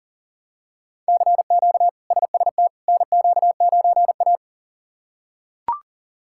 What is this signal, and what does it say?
Morse code, heard as a keyed tone, about 40 wpm, 700 Hz. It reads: CQ SST NQ9A
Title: K1USN SST QSO Practice - 40wpm